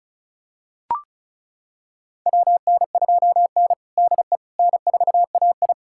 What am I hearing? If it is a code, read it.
Morse code, heard as a keyed tone, about 36 wpm, 700 Hz. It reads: WN2N DE N4AI
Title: USA Callsign DE Callsign - 4 Characters - 35wpm